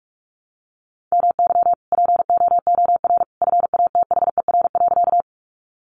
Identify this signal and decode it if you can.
Morse code, heard as a keyed tone, about 44 wpm, 700 Hz. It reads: MY POOR FATHER.
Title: Sentences from Top 1000 Words 45wpm